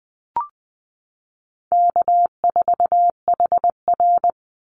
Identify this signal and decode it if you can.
Morse code, heard as a keyed tone, about 20 wpm, 700 Hz. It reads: K4HR
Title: Easy Call Signs Repeated 2x - 20wpm